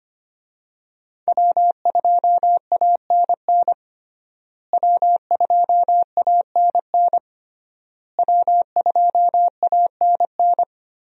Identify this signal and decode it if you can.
Morse code, heard as a keyed tone, about 25 wpm, 700 Hz. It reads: W2ANN W2ANN W2ANN
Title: USA Call Signs - 5 Characters - Repeated 3x - 25wpm